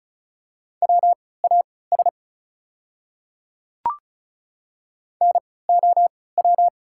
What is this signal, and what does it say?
Morse code, heard as a keyed tone, about 36 wpm, spaced as for 20 wpm, 690 Hz. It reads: WAS NOW
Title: 3x Character Spacing - Top 100 Words - 35wpm